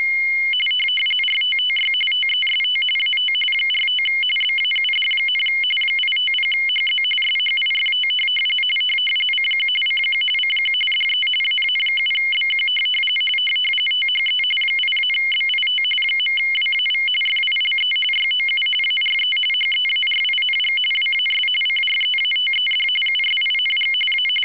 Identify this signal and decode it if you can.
RTTY, about 45.45 baud. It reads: RTTY TEST AT 45 BAUD, 850HZ SHIFT CQ CQ CQ DE AA7BQ THE 8-BIT .WAV FILE YOU ARE HEARING WAS CREATED USING A
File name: w ry850 / rtty45